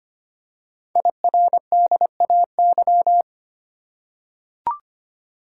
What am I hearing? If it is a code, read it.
Morse code, heard as a keyed tone, about 25 wpm, 700 Hz. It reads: IRDAY
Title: Code Groups - 5 Letters - 25wpm